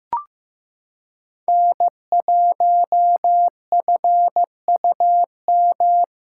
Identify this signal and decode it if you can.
Morse code, heard as a keyed tone, about 15 wpm, 700 Hz. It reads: N1FUM
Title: USA Callsign DE Callsign - 5 Characters - 15wpm